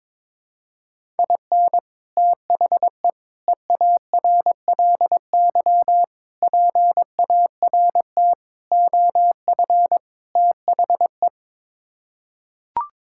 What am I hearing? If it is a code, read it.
Morse code, heard as a keyed tone, about 23 wpm, 700 Hz. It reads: IN THE EARLY PART OF THE
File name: practice / 22wpm / Sets of 6 Words - 22wpm